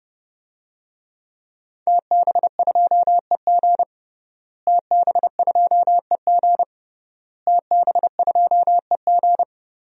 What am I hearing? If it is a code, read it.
Morse code, heard as a keyed tone, about 32 wpm, 700 Hz. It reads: TB2EG TB2EG TB2EG
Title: DXCC-100 Call Signs - 5 Characters - Repeated 3x - 30wpm